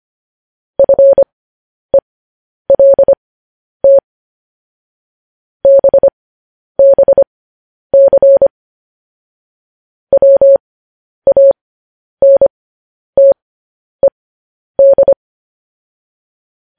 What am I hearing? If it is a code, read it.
Morse code, heard as a keyed tone, about 25 wpm, spaced as for 10 wpm, 550 Hz. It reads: FELT BBC WANTED
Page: News Headlines in Morse Code at 10 WPM